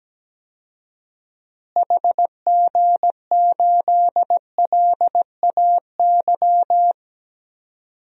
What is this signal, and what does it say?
Morse code, heard as a keyed tone, about 17 wpm, 700 Hz. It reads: HG8LAY
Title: DXCC-100 Call Signs - 6 Characters - Repeated 2x - 17wpm